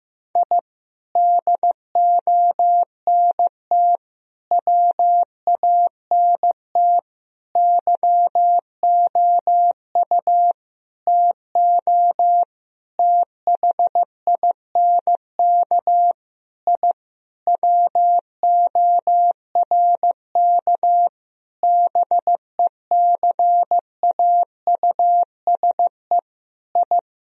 Morse code, heard as a keyed tone, about 15 wpm, 700 Hz. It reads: I DONT WANT YOU TO THINK I WORK BECAUSE I
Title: Sentences from Top 1000 Words - Review Part 3 - 15wpm